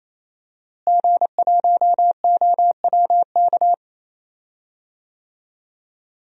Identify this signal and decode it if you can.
Morse code, heard as a keyed tone, about 27 wpm, 700 Hz. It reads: G1OWK
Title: DXCC-100 Call Signs - 5 Characters - Repeated 3x - 28wpm